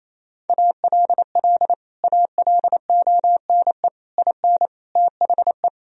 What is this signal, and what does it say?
Morse code, heard as a keyed tone, about 27 wpm, 700 Hz. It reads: ALL ALONE IN THE
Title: Sentences from Top 1000 Words - Review Part 2 - 28wpm